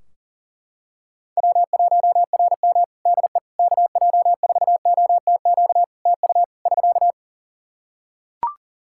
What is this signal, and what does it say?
Morse code, heard as a keyed tone, about 40 wpm, 700 Hz. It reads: W1RM DE KJ4OTQ TU <SK>
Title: Practice QSO - Exchange 4 - Ending - 40wpm